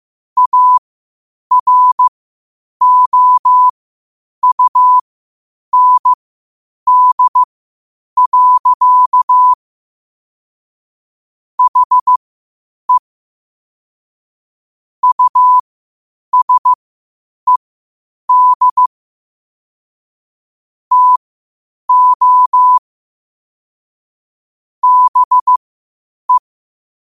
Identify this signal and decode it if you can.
Morse code, heard as a keyed tone, about 15 wpm, spaced as for 8 wpm, 1000 Hz. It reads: AROUND. HE USED TO BE
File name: qotdfiles / 8wpm